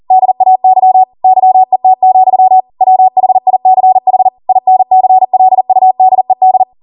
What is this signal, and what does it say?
Morse code, heard as a keyed tone, about 40 wpm, 750 Hz. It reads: DAY YET, WHICH INCLUDED